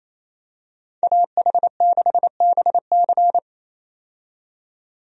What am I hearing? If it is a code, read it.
Morse code, heard as a keyed tone, about 28 wpm, 700 Hz. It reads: AH6BC
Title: USA Call Signs - 5 Characters - 28wpm